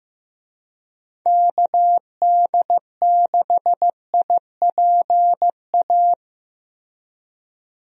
Morse code, heard as a keyed tone, about 15 wpm, 700 Hz. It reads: KD6IPA
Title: USA Call Signs - 6 Characters - Repeated 2x - 15wpm